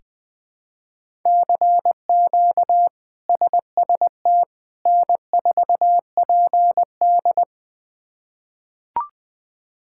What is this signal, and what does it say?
Morse code, heard as a keyed tone, about 20 wpm, 700 Hz. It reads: CQ SST N4PD
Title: K1USN SST QSO Practice - 20wpm